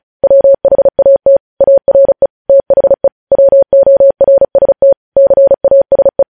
Morse code, heard as a keyed tone, about 36 wpm, 550 Hz. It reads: WHAT ARE THE WORST CASE
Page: news headlines in morse code at 35 wpm